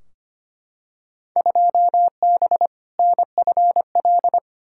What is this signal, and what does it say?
Morse code, heard as a keyed tone, about 25 wpm, 700 Hz. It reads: 2B NFL
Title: Field Day Exchanges Encore - 25wpm